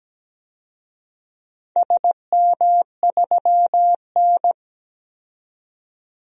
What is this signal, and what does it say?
Morse code, heard as a keyed tone, about 17 wpm, 700 Hz. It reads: SM3N
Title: DXCC-100 Call Signs - 4 Characters - 17wpm